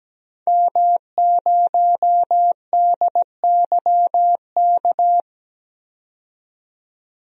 Morse code, heard as a keyed tone, about 17 wpm, 700 Hz. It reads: M0DYK
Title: DXCC-100 Call Signs - 5 Characters - 17wpm